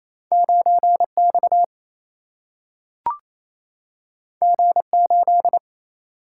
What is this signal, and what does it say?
Morse code, heard as a keyed tone, about 28 wpm, 700 Hz. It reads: 9 X G 8
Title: Single-Letter Single-Number 28wpm